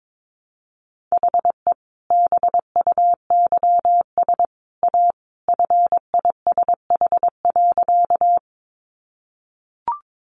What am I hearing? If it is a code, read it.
Morse code, heard as a keyed tone, about 22 wpm, 700 Hz. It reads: HE BUYS A FISH.
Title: Easy Phrases 22wpm